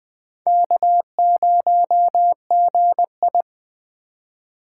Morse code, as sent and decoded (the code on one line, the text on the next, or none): -.- ----- --. ..
K0GI